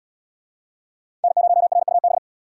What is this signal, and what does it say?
Morse code, heard as a keyed tone, about 70 wpm, 700 Hz. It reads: N0RUB